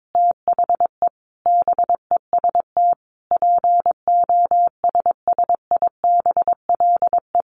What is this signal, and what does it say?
Morse code, heard as a keyed tone, about 22 wpm, 700 Hz. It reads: THE BEST POSSIBLE
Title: Sentences from Top 1300 Words 22wpm